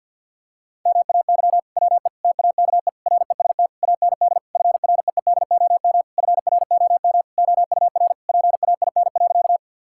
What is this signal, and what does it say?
Morse code, heard as a keyed tone, about 50 wpm, 700 Hz. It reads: MAY WE TAKE REST AND FREEDOM FROM OUR PAIN.
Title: Sentences from Top 1600 Words 50wpm